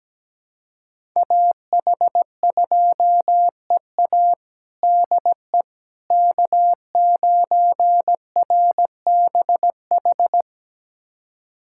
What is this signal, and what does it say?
Morse code, heard as a keyed tone, about 17 wpm, 700 Hz. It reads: AH2EA DE K9RBH
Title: USA Callsign DE Callsign - 5 Characters - 17wpm